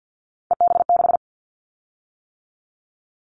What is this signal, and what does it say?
Morse code, heard as a keyed tone, about 50 wpm, 700 Hz. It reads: EB6
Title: Callsign Prefixes for Top 50 DXCC - 50wpm